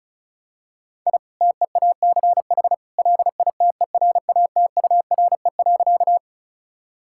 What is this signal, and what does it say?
Morse code, heard as a keyed tone, about 34 wpm, 700 Hz. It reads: I TEACH LITERATURE.
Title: Sentences from Top 1000 Words - Review Part 6 - 35wpm